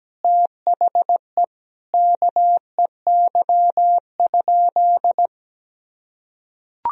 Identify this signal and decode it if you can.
Morse code, heard as a keyed tone, about 17 wpm, 700 Hz. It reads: THE KEY?
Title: Sentences from Top 1000 Words - Review Part 4 - 17wpm